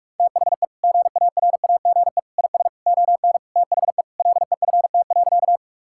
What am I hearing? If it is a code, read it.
Morse code, heard as a keyed tone, about 43 wpm, 700 Hz. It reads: THE GARAGE IS ON THE LEFT.
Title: Sentences from Top 1600 Words 45wpm